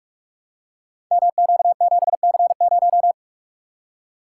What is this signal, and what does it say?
Morse code, heard as a keyed tone, about 45 wpm, 690 Hz. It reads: MQ7C0
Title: Code Groups - 5 Letter:Numbers - 45wpm